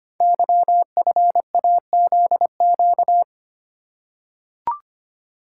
Morse code, heard as a keyed tone, about 25 wpm, 700 Hz. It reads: YFAZQ
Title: Code Groups - 5 Letter:Numbers - 25wpm